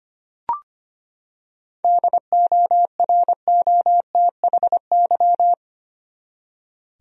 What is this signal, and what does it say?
Morse code, heard as a keyed tone, about 25 wpm, 700 Hz. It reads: DOROTHY
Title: USA Common Names - 25wpm